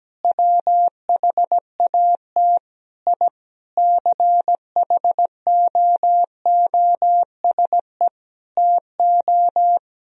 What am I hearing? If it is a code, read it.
Morse code, heard as a keyed tone, about 17 wpm, 700 Hz. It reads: WHAT I CHOOSE TO
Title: Sentences from Top 1000 Words - Review Part 4 - 17wpm